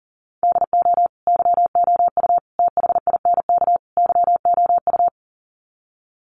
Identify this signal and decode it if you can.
Morse code, heard as a keyed tone, about 40 wpm, 700 Hz. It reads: DO YOU THINK YOU
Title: Sets of 4 Words 40wpm